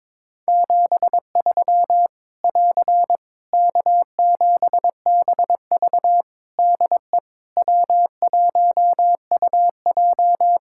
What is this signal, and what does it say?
Morse code, heard as a keyed tone, about 22 wpm, 700 Hz. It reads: 73 <AR> K7BV DE W1UJ
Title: K1USN SST QSO Practice 22wpm